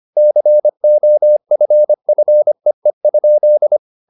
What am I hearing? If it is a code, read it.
Morse code, heard as a keyed tone, about 25 wpm, 600 Hz. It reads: COFFEE?
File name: the old man and the sea part 1 25wpm